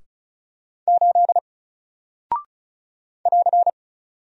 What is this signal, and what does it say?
Morse code, heard as a keyed tone, about 36 wpm, 700 Hz. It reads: 8 <AR>
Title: Single Character (Letter:Number:Punctuation:Prosign) Alt 35wpm